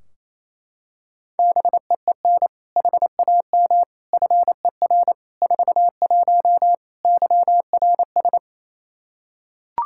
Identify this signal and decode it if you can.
Morse code, heard as a keyed tone, about 28 wpm, 700 Hz. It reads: BEEN HAM FER 41 YRS
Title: Practice QSO - Exchange 3 - More Info - 28wpm